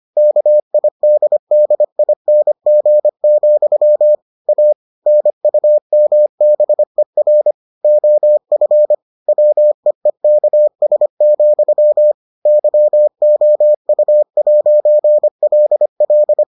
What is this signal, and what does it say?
Morse code, heard as a keyed tone, about 25 wpm, 600 Hz. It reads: KIDDING, A NUMBER OF WEEKS, YOU'LL